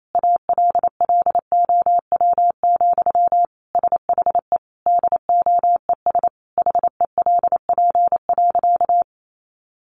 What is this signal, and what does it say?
Morse code, heard as a keyed tone, about 28 wpm, 700 Hz. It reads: ALLOW, SHE DOES HELP.
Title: Sentences from Top 700 Words 28wpm